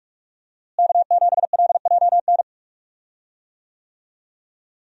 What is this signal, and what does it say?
Morse code, heard as a keyed tone, about 46 wpm, 700 Hz. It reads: K7LJN
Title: USA Call Signs - 5 Characters - Repeated 2x - 45wpm